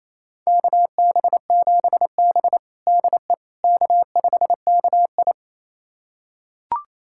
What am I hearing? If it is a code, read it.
Morse code, heard as a keyed tone, about 28 wpm, 700 Hz. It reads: KB7B DE K5KI
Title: USA Callsign DE Callsign - 4 Characters - 28wpm